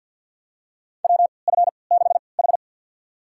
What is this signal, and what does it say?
Morse code, heard as a keyed tone, about 53 wpm, spaced as for 29 wpm, 700 Hz. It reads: WF6H
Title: USA Call Signs - 4 Characters - 3x Character Spacing - 50wpm